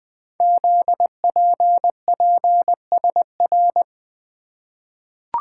Morse code, heard as a keyed tone, about 20 wpm, 700 Hz. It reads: ZPPSR
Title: Code Groups - 5 Letters - 20wpm